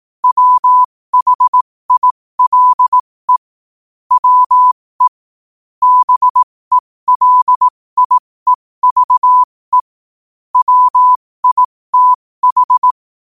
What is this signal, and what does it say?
Morse code, heard as a keyed tone, about 18 wpm, spaced as for 15 wpm, 1000 Hz. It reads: WHILE WE BELIEVE WITH
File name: qotdfiles / 15wpm